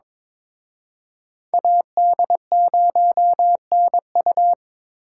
Morse code, heard as a keyed tone, about 26 wpm, 700 Hz. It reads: AD0NU
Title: Speed Racing - USA 2x2 Call Signs 22wpm